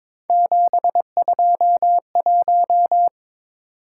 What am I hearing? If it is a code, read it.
Morse code, heard as a keyed tone, about 27 wpm, 700 Hz. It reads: 721
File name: Speed-Racing - Three Letter-Number Combinations - 22wpm